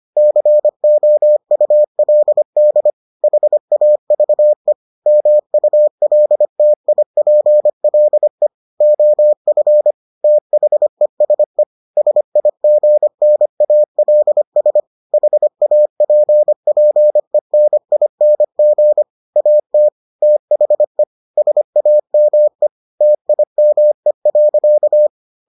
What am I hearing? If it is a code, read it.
Morse code, heard as a keyed tone, about 25 wpm, 600 Hz. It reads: COULD HAVE MULTIPLE OF THESE SIGNALS HAPPENING AT THE SAME TIME.